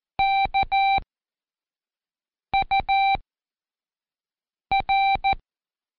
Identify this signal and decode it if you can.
Morse code, heard as a keyed tone, about 14 wpm, 780 Hz. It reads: K U R
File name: Seg29-Day4-CheckTest 4